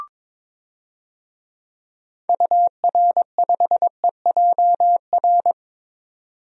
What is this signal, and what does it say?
Morse code, heard as a keyed tone, about 22 wpm, 700 Hz. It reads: UR5EJR
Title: DXCC-100 Call Signs - 6 Characters - 22wpm